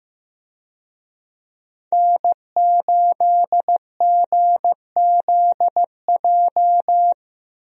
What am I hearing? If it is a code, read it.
Morse code, heard as a keyed tone, about 15 wpm, 700 Hz. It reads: N8GZJ
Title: USA Call Signs - 5 Characters - 15wpm